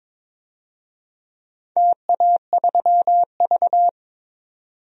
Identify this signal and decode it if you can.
Morse code, heard as a keyed tone, about 22 wpm, 700 Hz. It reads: TA3V
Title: DXCC-100 Call Signs - 4 Characters - 22wpm